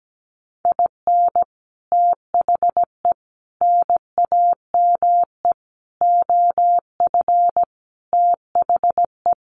Morse code, heard as a keyed tone, about 17 wpm, 700 Hz. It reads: IN THE NAME OF THE
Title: Sentences from Top 1500 Words 17wpm